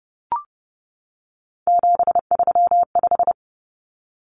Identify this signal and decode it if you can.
Morse code, heard as a keyed tone, about 30 wpm, 700 Hz. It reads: 735
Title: Common Cut Numbers - 3-Digit 30wpm